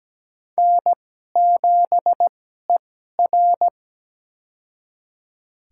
Morse code, heard as a keyed tone, about 17 wpm, spaced as for 12 wpm, 700 Hz. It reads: N7ER
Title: USA Call Signs - 4 Characters - 2x Character Spacing - 17wpm